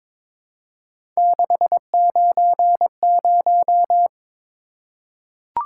Morse code, heard as a keyed tone, about 22 wpm, 700 Hz. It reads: 690
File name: practice / 22wpm / Three Letter-Number Combinations - 22wpm